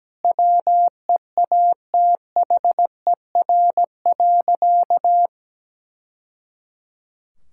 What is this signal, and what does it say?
Morse code, heard as a keyed tone, about 17 wpm, 700 Hz. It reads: WEATHER.
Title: Sentences from Top 1000 Words - Review Part 5 - 17wpm